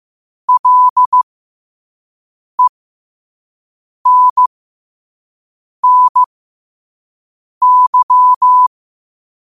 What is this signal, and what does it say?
Morse code, heard as a keyed tone, about 15 wpm, spaced as for 5 wpm, 1000 Hz. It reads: LENNY